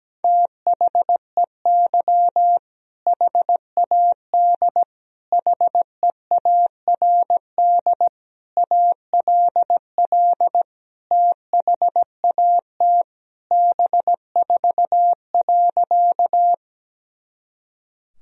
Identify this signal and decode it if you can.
Morse code, heard as a keyed tone, about 17 wpm, 700 Hz. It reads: THEY HAD HEARD ALL THAT B4.
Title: Sentences from Top 400 Words and Abbreviations 17wpm